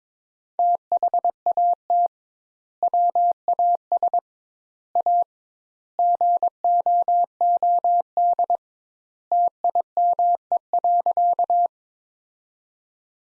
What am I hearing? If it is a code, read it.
Morse code, heard as a keyed tone, about 21 wpm, 700 Hz. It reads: THAT WAS A GOOD TIME.
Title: Sentences Top 200 Words - 22wpm